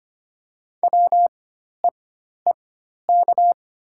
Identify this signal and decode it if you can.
Morse code, heard as a keyed tone, about 25 wpm, spaced as for 12 wpm, 690 Hz. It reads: WEEK
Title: 4x Character Spacing - Top 500 Words - 25wpm